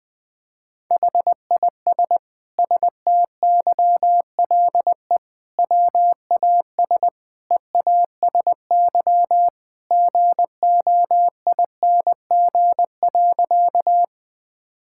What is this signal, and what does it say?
Morse code, heard as a keyed tone, about 20 wpm, 700 Hz. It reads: HIS STYLE WAS EASY GOING.
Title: Sentences from Top 1000 Words - Review Part 5 - 20wpm